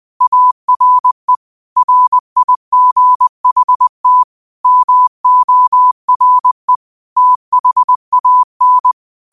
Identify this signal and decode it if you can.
Morse code, heard as a keyed tone, about 20 wpm, 1000 Hz. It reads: ARE RIGHT MORE THAN
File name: qotdfiles / 20wpm